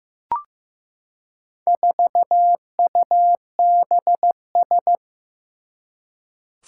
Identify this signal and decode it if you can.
Morse code, heard as a keyed tone, about 15 wpm, 700 Hz. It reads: 4UBS
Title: Four Challenging Letter-Number Combinations 15wpm